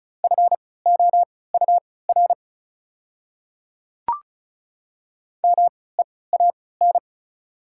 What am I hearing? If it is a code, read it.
Morse code, heard as a keyed tone, about 34 wpm, spaced as for 20 wpm, 700 Hz. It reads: FOUR MEAN
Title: 3x Character Spacing - Top 300 Words - 35wpm